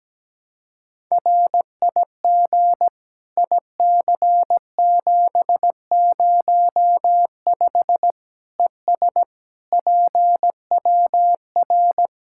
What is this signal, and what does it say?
Morse code, heard as a keyed tone, about 17 wpm, 700 Hz. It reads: RIG IC705 ES PWR
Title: Practice QSO - Exchange 2 - Info - 17wpm